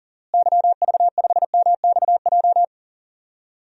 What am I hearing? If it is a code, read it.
Morse code, heard as a keyed tone, about 40 wpm, 700 Hz. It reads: YV5MXJ
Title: DXCC-100 Call Signs - 6 Characters - Repeated 2x - 40wpm